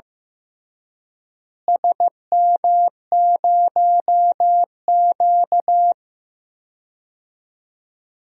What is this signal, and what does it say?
Morse code, heard as a keyed tone, about 16 wpm, 700 Hz. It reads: SM0Q
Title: DXCC-100 Call Signs - 4 Characters - Repeated 3x - 15wpm